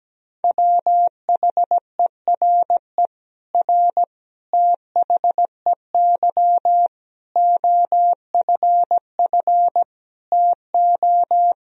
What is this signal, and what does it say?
Morse code, heard as a keyed tone, about 17 wpm, 700 Hz. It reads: WHERE R THEY OFF TO
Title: Sentences from Top 200 Words and Abbreviations 17wpm